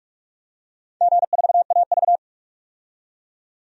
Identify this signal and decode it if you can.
Morse code, heard as a keyed tone, about 45 wpm, 700 Hz. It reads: G4AV
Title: DXCC-100 Call Signs - 4 Characters - 45wpm